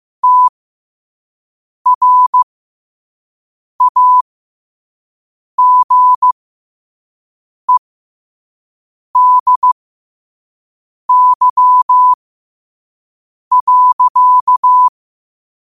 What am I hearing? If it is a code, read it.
Morse code, heard as a keyed tone, about 15 wpm, spaced as for 5 wpm, 1000 Hz. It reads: TRAGEDY.
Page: New quotes every day in morse code at 5 Words per minute.